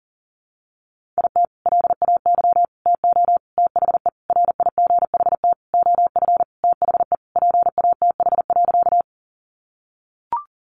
Morse code, heard as a keyed tone, about 40 wpm, 700 Hz. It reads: IT LAY TO THE RIGHT OF THE PATH.
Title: Sentences from Top 1400 Words 40wpm